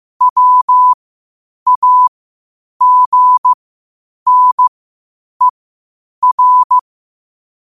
Morse code, heard as a keyed tone, about 15 wpm, spaced as for 8 wpm, 1000 Hz. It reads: WAGNER
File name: qotdfiles / 8wpm